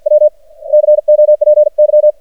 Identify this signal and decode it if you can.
Morse code, heard as a keyed tone, about 58 wpm, spaced as for 50 wpm, 600 Hz. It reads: W COPY
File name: archived / CRSnet